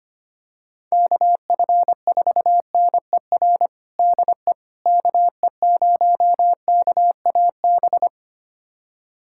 Morse code, heard as a keyed tone, about 25 wpm, 700 Hz. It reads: KF4NER DE KE0KAB
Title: USA Callsign DE Callsign 25wpm